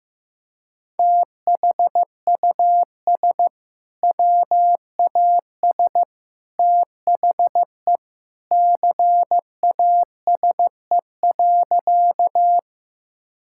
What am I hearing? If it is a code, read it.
Morse code, heard as a keyed tone, about 15 wpm, 700 Hz. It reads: THUS WAS THE CASE.
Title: Sentences from Top 1000 Words - Review Part 5 - 15wpm